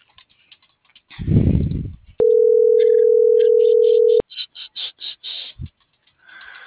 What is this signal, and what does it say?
Ringback tone (North America), 440 + 480 Hz.